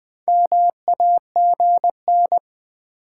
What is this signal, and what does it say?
Morse code, heard as a keyed tone, about 20 wpm, 700 Hz. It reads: MAGN
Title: Build A Word - 4 Letters At A Time - 11-Letter Words - 20wpm